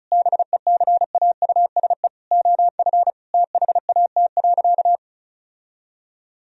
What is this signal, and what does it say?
Morse code, heard as a keyed tone, about 36 wpm, 700 Hz. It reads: BECAUSE OF THAT.
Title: Sentences from Top 1000 Words - Review Part 2 - 35wpm